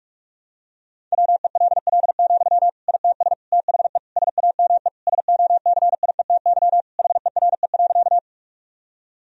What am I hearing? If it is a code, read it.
Morse code, heard as a keyed tone, about 46 wpm, 700 Hz. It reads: WELL, ITS THE SAME SOCIETY HERE.
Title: Sentences from Top 1000 Words - Review Part 4 - 45wpm